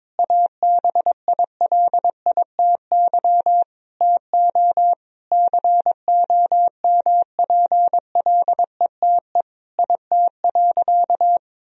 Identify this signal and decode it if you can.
Morse code, heard as a keyed tone, about 22 wpm, 700 Hz. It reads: ABILITY TO COMPLETE IT.
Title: Sentences from Top 1000 Words - Review Part 3 - 22wpm